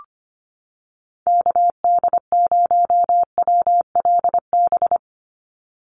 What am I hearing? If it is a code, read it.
Morse code, heard as a keyed tone, about 25 wpm, 700 Hz. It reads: KD0WLB
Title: USA Call Signs 25wpm